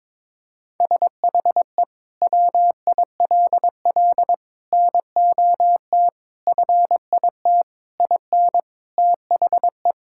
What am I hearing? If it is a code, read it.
Morse code, heard as a keyed tone, about 22 wpm, 700 Hz. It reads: SHE WILL NOT FIT IN THE
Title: Sentences from Top 1000 Words - Review Part 5 - 22wpm